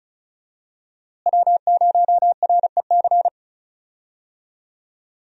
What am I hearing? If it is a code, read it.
Morse code, heard as a keyed tone, about 34 wpm, 700 Hz. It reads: W0REC
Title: USA Call Signs - 5 Characters - Repeated 3x - 35wpm